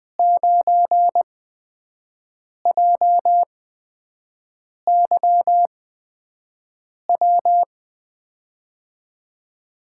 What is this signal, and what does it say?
Morse code, heard as a keyed tone, about 21 wpm, spaced as for 6 wpm, 700 Hz. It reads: 9JYW
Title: USA Call Signs - 8x Character Spacing - 20wpm